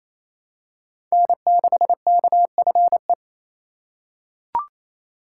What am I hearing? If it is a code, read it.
Morse code, heard as a keyed tone, about 28 wpm, 700 Hz. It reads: N6KFE
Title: USA Call Signs - 5 Characters - 4x Character Spacing - 28wpm